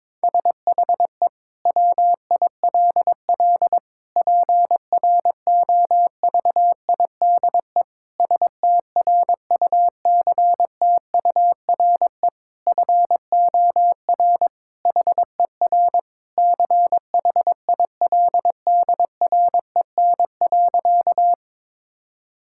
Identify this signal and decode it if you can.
Morse code, heard as a keyed tone, about 22 wpm, 700 Hz. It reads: SHE WILL PROVIDE STRUCTURE FOR HER CHILDREN.
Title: Sentences from Top 1000 Words - Review Part 6 - 22wpm